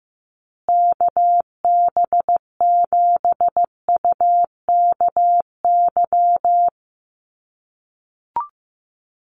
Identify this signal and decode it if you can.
Morse code, heard as a keyed tone, about 15 wpm, 700 Hz. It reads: KB7UKY
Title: USA Call Signs Repeated 2x 15wpm